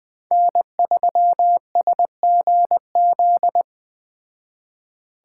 Morse code, heard as a keyed tone, about 20 wpm, 700 Hz. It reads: N3SGZ
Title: Speed Racing - USA 1x3 Call Signs 17wpm